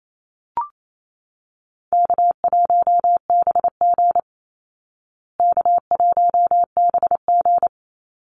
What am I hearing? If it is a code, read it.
Morse code, heard as a keyed tone, about 28 wpm, 700 Hz. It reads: K1BG K1BG
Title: Easy Call Signs Repeated 3x - 28wpm